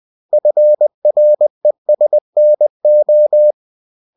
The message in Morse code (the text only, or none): FRESNO